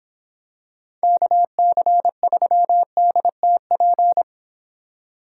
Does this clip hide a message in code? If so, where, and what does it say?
Yes: Morse code from 1.0 to 4.2 s, heard as a keyed tone, about 26 wpm, 700 Hz. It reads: KC3DTP